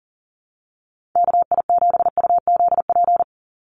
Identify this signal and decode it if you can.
Morse code, heard as a keyed tone, about 38 wpm, 700 Hz. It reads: KI7UZP
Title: USA Call Signs 40wpm